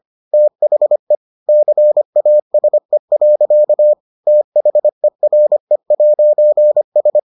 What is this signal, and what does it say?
Morse code, heard as a keyed tone, about 25 wpm, 600 Hz. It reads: THE CASE. THERE'S